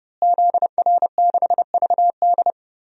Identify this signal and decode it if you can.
Morse code, heard as a keyed tone, about 31 wpm, 700 Hz. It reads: ZR6VD
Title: DXCC-100 Call Signs - 5 Characters - Repeated 2x - 30wpm